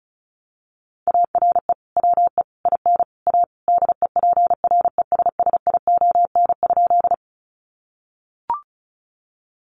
Morse code, heard as a keyed tone, about 34 wpm, 700 Hz. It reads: ARE WE IN A DEPRESSION?
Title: Sentences from Top 1300 Words 35wpm